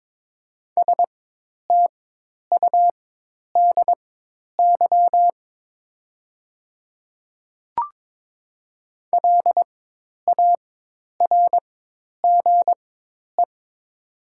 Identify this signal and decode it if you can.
Morse code, heard as a keyed tone, about 22 wpm, spaced as for 10 wpm, 700 Hz. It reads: STUDY LARGE
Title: 4x Character Spacing - Top 300 Words - 22wpm